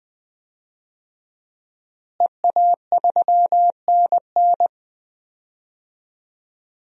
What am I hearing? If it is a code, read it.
Morse code, heard as a keyed tone, about 20 wpm, 700 Hz. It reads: EA3NN
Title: DXCC-100 Call Signs - 5 Characters - Repeated 2x - 20wpm